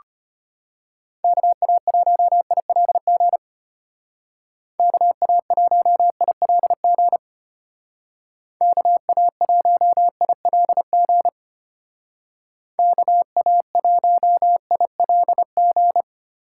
Morse code, heard as a keyed tone, about 30 wpm, 700 Hz. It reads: KA1ILG KA1ILG KA1ILG KA1ILG
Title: Speed Racing - USA 2x3 Call Signs 25wpm